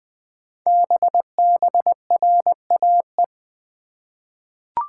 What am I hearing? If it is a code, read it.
Morse code, heard as a keyed tone, about 20 wpm, 700 Hz. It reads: BBRAE
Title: Code Groups - 5 Letters - 20wpm